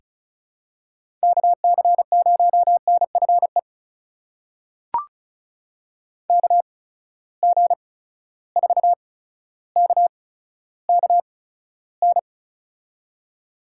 Morse code, heard as a keyed tone, about 36 wpm, spaced as for 15 wpm, 700 Hz. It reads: KC0NFE KG4KKN
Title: USA Call Signs - 6 Characters - 8x Character Spacing - 35wpm